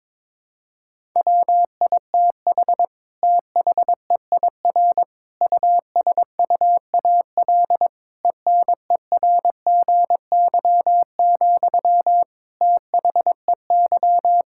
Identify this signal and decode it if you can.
Morse code, heard as a keyed tone, about 22 wpm, 700 Hz. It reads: WITH THEIR USUAL ENERGY, THEY
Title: Sentences from Top 1000 Words - Review Part 6 - 22wpm